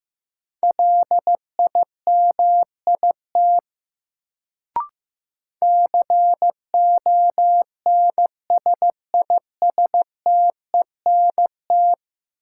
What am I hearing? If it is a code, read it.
Morse code, heard as a keyed tone, about 16 wpm, 700 Hz. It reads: LIMIT CONSISTENT
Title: Sentences from Top 1500 Words - New Word Drill Alt - 15wpm